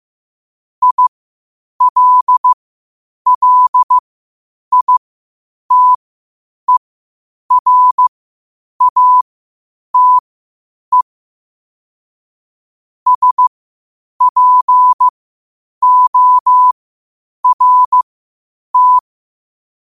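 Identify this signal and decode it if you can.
Morse code, heard as a keyed tone, about 15 wpm, spaced as for 8 wpm, 1000 Hz. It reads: ILLITERATE SPORT
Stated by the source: Morse code at 8 wpm.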